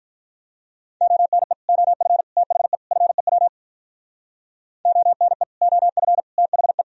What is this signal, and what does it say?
Morse code, heard as a keyed tone, about 46 wpm, 700 Hz. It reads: ONE OF THE FEW ONE OF THE